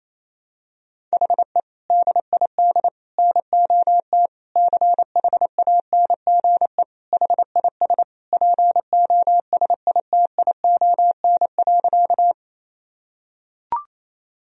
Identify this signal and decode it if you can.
Morse code, heard as a keyed tone, about 28 wpm, 700 Hz. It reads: HE DID NOT CHANGE HIS POSITION.
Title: Sentences from Top 1000 Words - Review Part 1 - 28wpm